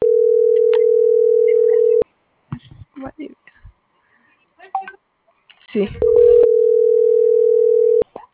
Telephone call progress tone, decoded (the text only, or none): ringback tone